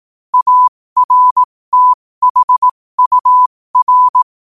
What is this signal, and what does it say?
Morse code, heard as a keyed tone, about 18 wpm, spaced as for 15 wpm, 1000 Hz. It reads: ARTHUR
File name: qotdfiles / 15wpm